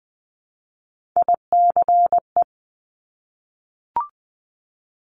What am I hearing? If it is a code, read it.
Morse code, heard as a keyed tone, about 20 wpm, 700 Hz. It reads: ICE
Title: Multi-letter Phonemes Repeated 3x 20wpm